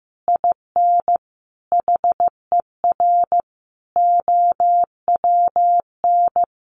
Morse code, heard as a keyed tone, about 15 wpm, 700 Hz. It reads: IN HER OWN
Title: Sentences from Top 300 Words 15wpm